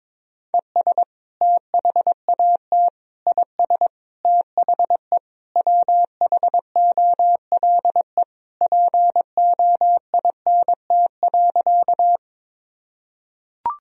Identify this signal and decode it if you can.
Morse code, heard as a keyed tone, about 21 wpm, 700 Hz. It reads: ES THAT IS THE WHOLE POINT.
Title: Sentences from Top 400 Words and Abbreviations 22wpm